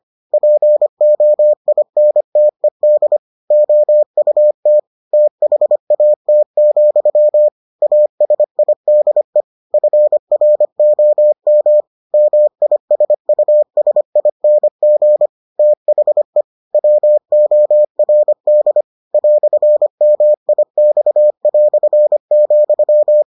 Morse code, heard as a keyed tone, about 25 wpm, 600 Hz. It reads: POINTED OUT THAT, ASIDE FROM MISUSING THE WORD "MIX",